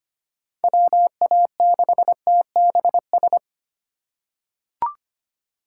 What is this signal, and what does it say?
Morse code, heard as a keyed tone, about 25 wpm, 700 Hz. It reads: WA6TBS
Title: USA Call Signs - 6 Characters - 4x Character Spacing - 25wpm